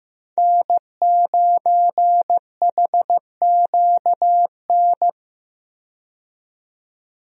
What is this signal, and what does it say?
Morse code, heard as a keyed tone, about 15 wpm, 700 Hz. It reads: N9HQN
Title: USA Call Signs - 5 Characters - 15wpm